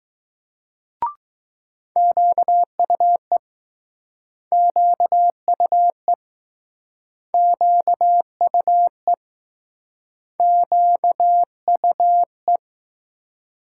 Morse code, heard as a keyed tone, about 20 wpm, 700 Hz. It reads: QUE QUE QUE QUE